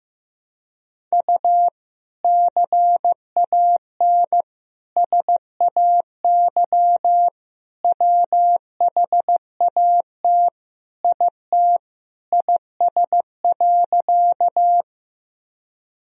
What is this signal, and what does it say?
Morse code, heard as a keyed tone, about 15 wpm, 700 Hz. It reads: U CAN SAY WHAT IT IS.
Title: Sentences from Top 200 Words and Abbreviations 15wpm